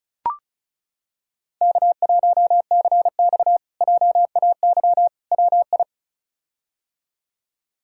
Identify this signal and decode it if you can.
Morse code, heard as a keyed tone, about 34 wpm, 700 Hz. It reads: K1CX JAY WI
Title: K1USN SST QSO Practice - 35wpm